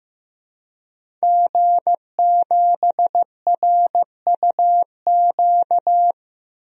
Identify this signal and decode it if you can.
Morse code, heard as a keyed tone, about 15 wpm, 700 Hz. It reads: G7RUQ
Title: DXCC-100 Call Signs - 5 Characters - Repeated 2x - 15wpm